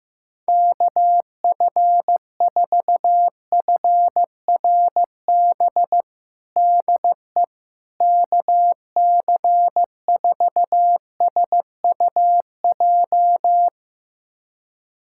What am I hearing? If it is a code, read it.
Morse code, heard as a keyed tone, about 16 wpm, 700 Hz. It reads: KF4FRB DE KC4SUJ
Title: USA Callsign DE Callsign - 6 Characters - 15wpm